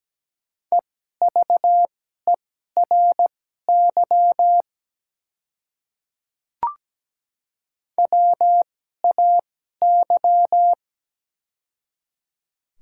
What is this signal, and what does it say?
Morse code, heard as a keyed tone, about 17 wpm, spaced as for 12 wpm, 700 Hz. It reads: EVERY WAY
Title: 2x Character Spacing - Top 300 Words - 17wpm